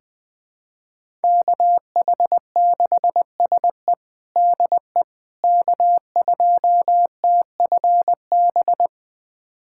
Morse code, heard as a keyed tone, about 20 wpm, 700 Hz. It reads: KH6SE DE K2TFB
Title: USA Callsign DE Callsign - 5 Characters - 20wpm